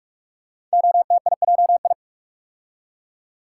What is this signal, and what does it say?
Morse code, heard as a keyed tone, about 44 wpm, 700 Hz. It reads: OTIJI
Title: Code Groups - 5 Letters - 45wpm